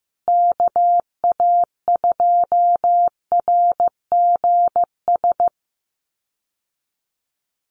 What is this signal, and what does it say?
Morse code, heard as a keyed tone, about 15 wpm, 700 Hz. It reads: KA2RGS
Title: USA Call Signs Repeated 3x 15wpm